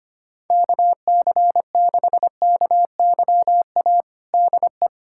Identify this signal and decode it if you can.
Morse code, heard as a keyed tone, about 25 wpm, 700 Hz. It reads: KC6KYA DE
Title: USA Callsign DE Callsign 25wpm